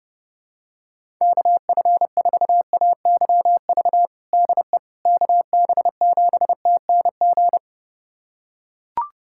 Morse code, heard as a keyed tone, about 30 wpm, 700 Hz. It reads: KF4AYV DE KB7TNG
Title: USA Callsign DE Callsign - 6 Characters - 30wpm